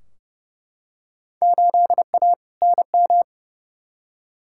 Morse code, heard as a keyed tone, about 30 wpm, 700 Hz. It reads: 8A NM
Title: Field Day Exchanges Encore - 30wpm